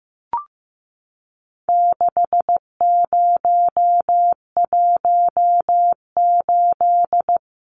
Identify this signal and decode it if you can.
Morse code, heard as a keyed tone, about 15 wpm, 700 Hz. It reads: 6018
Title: Numbers - 4-Digit 15wpm